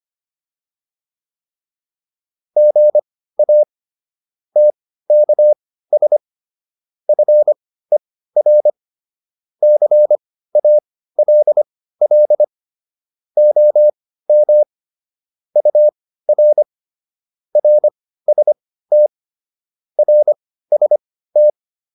Morse code, heard as a keyed tone, about 25 wpm, spaced as for 15 wpm, 600 Hz. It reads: GA TKS FER CALL OM UR RST RST
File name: QSO 408 15